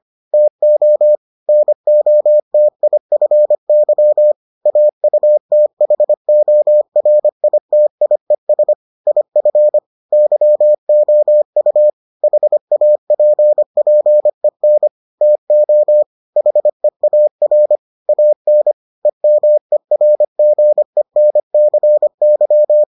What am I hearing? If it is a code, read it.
Morse code, heard as a keyed tone, about 25 wpm, 600 Hz. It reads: TO NOTIFY AUTHORITIES IF YOU HAPPEN TO HEAR AN EMERGENCY